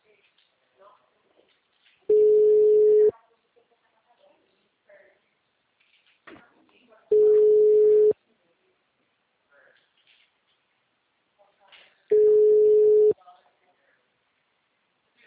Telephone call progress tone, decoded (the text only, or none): ringback tone